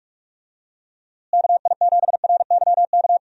Morse code, heard as a keyed tone, about 46 wpm, 700 Hz. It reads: KI7RYK
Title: USA Call Signs - 6 Characters - Repeated 2x - 45wpm